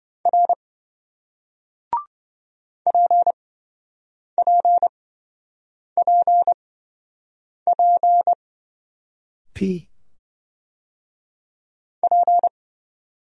Morse code, heard as a keyed tone, about 26 wpm, spaced as for 7 wpm, 700 Hz. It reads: R PPPP P